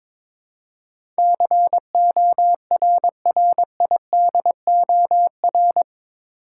Morse code, heard as a keyed tone, about 22 wpm, 700 Hz. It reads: CORRIDOR
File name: practice / 22wpm / Eight Letter Words - 22wpm